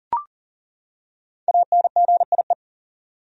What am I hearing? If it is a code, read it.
Morse code, heard as a keyed tone, about 40 wpm, 700 Hz. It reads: ANGIE
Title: USA Common Names - Max 5 Letters - 40wpm